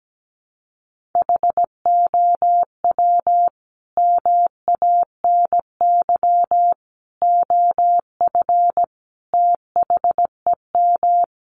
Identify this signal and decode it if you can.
Morse code, heard as a keyed tone, about 17 wpm, 700 Hz. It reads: HOW MANY OF THEM
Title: Sentences from Top 1200 Words 17wpm